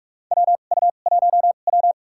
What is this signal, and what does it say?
Morse code, heard as a keyed tone, about 45 wpm, spaced as for 32 wpm, 700 Hz. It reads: WU1W
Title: USA Call Signs - 2x Character Spacing - 45wpm